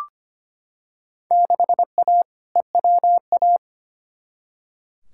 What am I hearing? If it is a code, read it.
Morse code, heard as a keyed tone, about 25 wpm, 700 Hz. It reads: 6A EWA
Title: Field Day Exchanges Encore - 25wpm